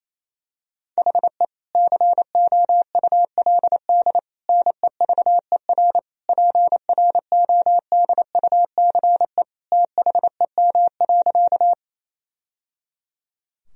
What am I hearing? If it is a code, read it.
Morse code, heard as a keyed tone, about 27 wpm, 700 Hz. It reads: HE COULD NEVER PRODUCE THEM.
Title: Sentences from Top 1000 Words - Review Part 3 - 28wpm